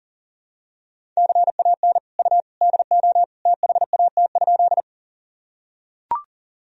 Morse code, heard as a keyed tone, about 37 wpm, 700 Hz. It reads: CAN U DO THAT?
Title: Sentences from Top 100 Words and Abbreviations 40wpm